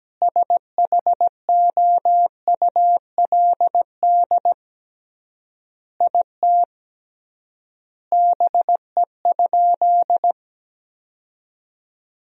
Morse code, heard as a keyed tone, about 17 wpm, 700 Hz. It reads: SHOULD IT BE?
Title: 3x Word-Spacing - Sentences from Top 300 Words - 17wpm